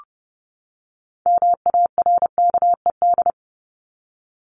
Morse code, heard as a keyed tone, about 30 wpm, 700 Hz. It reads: MARKED
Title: ED Suffix - Top 100 Words with Base Words 30wpm